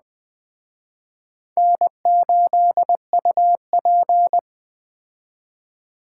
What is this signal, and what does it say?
Morse code, heard as a keyed tone, about 20 wpm, 700 Hz. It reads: N8UP
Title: Speed Racing - USA Call Signs 17wpm